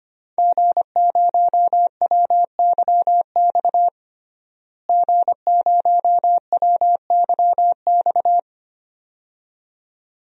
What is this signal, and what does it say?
Morse code, heard as a keyed tone, about 25 wpm, 700 Hz. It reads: G0WYX G0WYX
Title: DXCC-100 Call Signs - 5 Characters - Repeated 2x - 25wpm